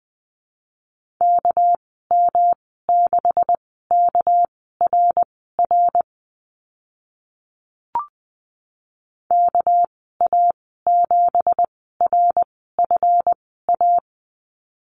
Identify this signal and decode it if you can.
Morse code, heard as a keyed tone, about 20 wpm, spaced as for 14 wpm, 700 Hz. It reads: KM6KRR KA7RFA
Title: 207 Lesson 33g - Letter j - Call Signs 20wpm